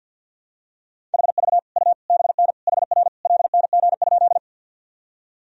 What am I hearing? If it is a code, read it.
Morse code, heard as a keyed tone, about 53 wpm, 700 Hz. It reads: HV U BN HR LNG?
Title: Sentences from Top 200 Words and Abbreviations 50wpm